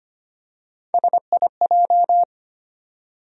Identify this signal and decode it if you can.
Morse code, heard as a keyed tone, about 25 wpm, 700 Hz. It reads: SIJ
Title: USA Call Sign Postfixes - 3 Letters 25wpm